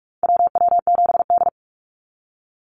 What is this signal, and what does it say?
Morse code, heard as a keyed tone, about 44 wpm, 700 Hz. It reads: WW7D
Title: NA SOTA Activator Callsigns 45wpm